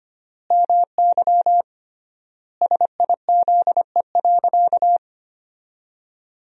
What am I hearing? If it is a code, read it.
Morse code, heard as a keyed tone, about 25 wpm, 700 Hz. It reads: MY SIZE.
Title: 3x Word-Spacing - Sentences from Top 400 Words - 25wpm